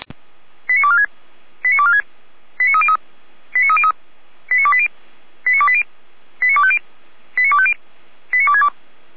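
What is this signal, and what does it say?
Five-tone selective call (ZVEI), heard as quick runs of five tones, tones of about 80 ms. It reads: 89246 89246 89303 89303 89280 89280 89250 89250 89262
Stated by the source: ZVEI five-tone selective call received on SDR.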